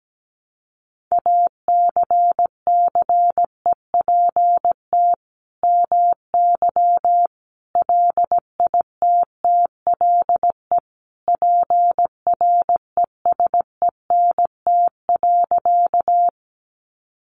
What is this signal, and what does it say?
Morse code, heard as a keyed tone, about 17 wpm, 700 Hz. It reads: ACCEPT MY LITTLE PRESENT.
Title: Sentences from Top 1300 Words 17wpm